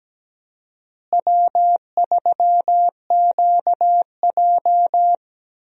Morse code, heard as a keyed tone, about 17 wpm, 700 Hz. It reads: W3QJ
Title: USA Call Signs - 4 Characters - Repeated 3x - 17wpm